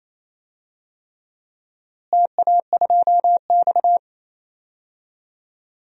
Morse code, heard as a keyed tone, about 28 wpm, 700 Hz. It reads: TA2X
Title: DXCC-100 Call Signs - 4 Characters - Repeated 3x - 28wpm